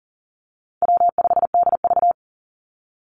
Morse code, heard as a keyed tone, about 40 wpm, 700 Hz. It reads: W5DV
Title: Easy Call Signs Repeated 3x - 40wpm